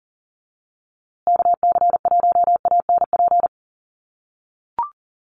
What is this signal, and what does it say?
Morse code, heard as a keyed tone, about 40 wpm, 700 Hz. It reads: KC1ANP
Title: USA Call Signs Repeated 2x 40wpm